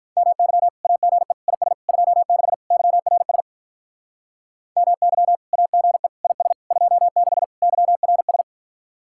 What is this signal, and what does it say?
Morse code, heard as a keyed tone, about 44 wpm, 700 Hz. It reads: MY AGE IS 26 YRS MY AGE IS 26 YRS